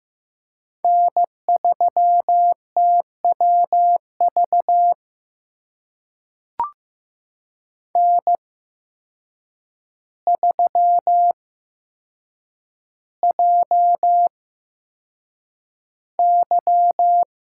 Morse code, heard as a keyed tone, about 15 wpm, spaced as for 7 wpm, 700 Hz. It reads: N3TWV N3JY
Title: USA Call Signs - 5 Characters - 8x Character Spacing - 15wpm